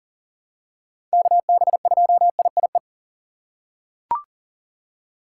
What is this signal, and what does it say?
Morse code, heard as a keyed tone, about 38 wpm, 700 Hz. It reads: KB2IIE
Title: USA Call Signs - 6 Characters - 3x Character Spacing - 40wpm